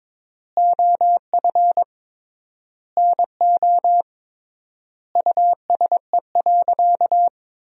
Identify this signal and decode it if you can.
Morse code, heard as a keyed tone, about 22 wpm, 700 Hz. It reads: OF NO USE.
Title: 3x Word-Spacing - Sentences from Top 100 Words - 22wpm